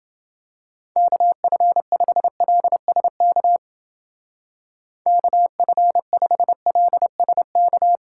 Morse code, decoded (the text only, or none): KF5LSK KF5LSK